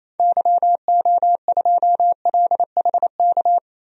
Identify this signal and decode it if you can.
Morse code, heard as a keyed tone, about 28 wpm, 700 Hz. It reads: YO2LHK
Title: DXCC-100 Call Signs - 6 Characters - 28wpm